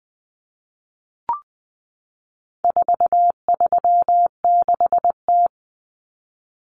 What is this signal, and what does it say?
Morse code, heard as a keyed tone, about 20 wpm, 700 Hz. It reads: 436T
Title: Common Cut Numbers - 4-Digit 20wpm